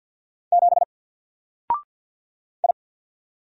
Morse code, heard as a keyed tone, about 50 wpm, 700 Hz. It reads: 7 I
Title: Single Letter-Number Alt 50wpm